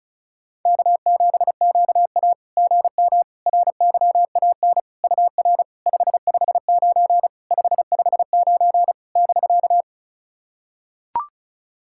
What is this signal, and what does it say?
Morse code, heard as a keyed tone, about 36 wpm, 700 Hz. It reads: K7QA GM RYAN UR 559 559 <BK>
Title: SOTA QSO Practice 35wpm